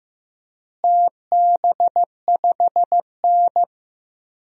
Morse code, heard as a keyed tone, about 15 wpm, 700 Hz. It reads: TB5N
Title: DXCC-100 Call Signs - 4 Characters - Repeated 3x - 15wpm